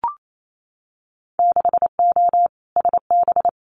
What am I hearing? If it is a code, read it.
Morse code, heard as a keyed tone, about 28 wpm, 690 Hz. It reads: 6O SB
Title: Winter Field Day Exchanges Encore 28wpm